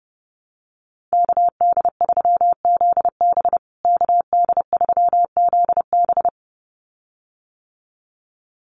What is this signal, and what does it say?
Morse code, heard as a keyed tone, about 30 wpm, 700 Hz. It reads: KD3ZB KD3ZB
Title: USA Call Signs Repeated 2x 30wpm